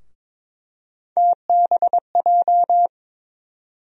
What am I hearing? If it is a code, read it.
Morse code, heard as a keyed tone, about 22 wpm, 700 Hz. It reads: TBJ
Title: USA Call Sign Postfixes - 3 Letters 22wpm